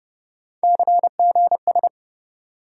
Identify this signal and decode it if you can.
Morse code, heard as a keyed tone, about 31 wpm, 700 Hz. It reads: CGS
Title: USA Call Sign Postfixes - 3 Letters 30wpm